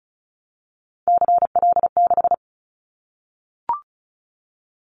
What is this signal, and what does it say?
Morse code, heard as a keyed tone, about 34 wpm, 700 Hz. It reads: CL6
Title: Callsign Prefixes for Top 100 DXCC - 35wpm